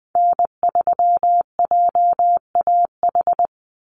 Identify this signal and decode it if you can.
Morse code, heard as a keyed tone, about 20 wpm, 700 Hz. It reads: N3JAH
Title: USA Call Signs 20wpm